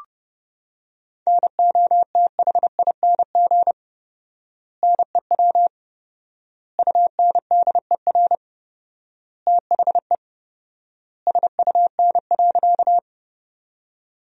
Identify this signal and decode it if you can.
Morse code, heard as a keyed tone, about 31 wpm, 700 Hz. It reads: NOTHING NEW UNDER THE SUN.
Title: 4x Word-Spacing - Sentences from Top 500 Words - 30wpm